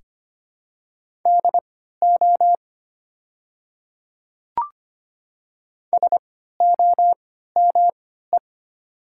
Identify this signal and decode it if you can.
Morse code, heard as a keyed tone, about 25 wpm, spaced as for 14 wpm, 690 Hz. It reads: DO SOME
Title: 3x Character Spacing - Top 200 Words - 25wpm